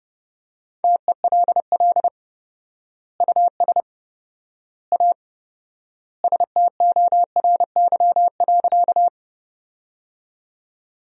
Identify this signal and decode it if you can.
Morse code, heard as a keyed tone, about 31 wpm, 700 Hz. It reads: TELL US A STORY.
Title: 4x Word-Spacing - Sentences from Top 300 Words - 30wpm